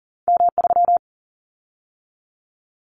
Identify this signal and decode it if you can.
Morse code, heard as a keyed tone, about 40 wpm, 700 Hz. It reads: M3
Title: Callsign Prefixes for Top 25 DXCC - 40wpm